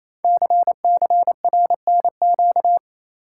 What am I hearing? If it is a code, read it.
Morse code, heard as a keyed tone, about 28 wpm, 700 Hz. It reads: CCRNQ
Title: Code Groups - 5 Letter:Numbers - 28wpm